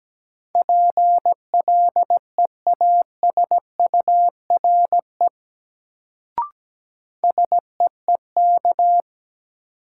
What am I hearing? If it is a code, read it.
Morse code, heard as a keyed tone, about 17 wpm, 700 Hz. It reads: PLEASURE SEEK
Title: Sentences from Top 1600 Words - New Word Drill - 17wpm